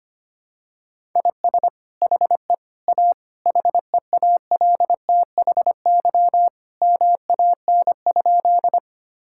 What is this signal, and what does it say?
Morse code, heard as a keyed tone, about 25 wpm, 700 Hz. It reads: IS HE A HEALTHY MAN?
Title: Sentences from Top 1000 Words - Review Part 5 - 25wpm